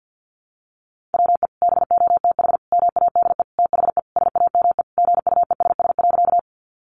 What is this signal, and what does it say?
Morse code, heard as a keyed tone, about 50 wpm, 700 Hz. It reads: WE BOTH MADE THE SAME GUESS.
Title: Sentences from Top 1500 Words 50wpm